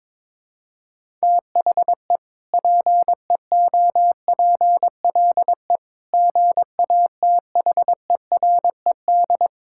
Morse code, heard as a keyed tone, about 22 wpm, 700 Hz. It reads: THE PEOPLE GATHERED
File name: practice / 22wpm / Sentences with 2-Word Phrases - 22wpm